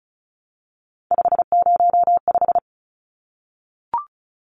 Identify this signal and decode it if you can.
Morse code, heard as a keyed tone, about 34 wpm, 700 Hz. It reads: 505
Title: Numbers - 3-Digit 35wpm